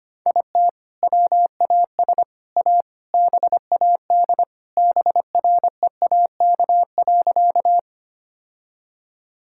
Morse code, heard as a keyed tone, about 25 wpm, 700 Hz. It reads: IT WAS A BAD BREAK.
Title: Sentences from Top 1000 Words - Review Part 6 - 25wpm